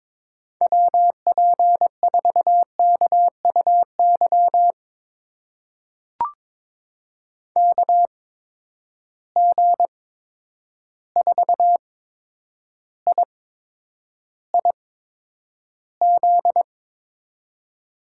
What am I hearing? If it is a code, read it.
Morse code, heard as a keyed tone, about 22 wpm, spaced as for 9 wpm, 700 Hz. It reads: WP4KUY KG4IIZ
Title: USA Call Signs - 6 Characters - 8x Character Spacing - 22wpm